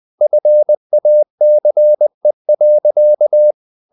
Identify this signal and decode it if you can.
Morse code, heard as a keyed tone, about 20 wpm, 600 Hz. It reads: FACE.